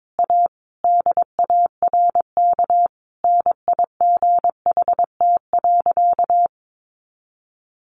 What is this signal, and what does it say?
Morse code, heard as a keyed tone, about 22 wpm, 700 Hz. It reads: A DARK NIGHT.
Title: Sentences from Top 500 Words 22wpm